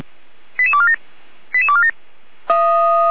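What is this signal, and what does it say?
Five-tone selective call (ZVEI), heard as quick runs of five tones, tones of about 70 ms. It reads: 88247 88247
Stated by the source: Five-tone ZVEI selective call received by SDR.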